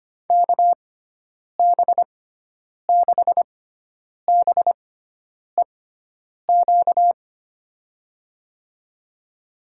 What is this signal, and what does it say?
Morse code, heard as a keyed tone, about 25 wpm, spaced as for 9 wpm, 700 Hz. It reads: KB6BEQ
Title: USA Call Signs - 6x Character Spacing - 25wpm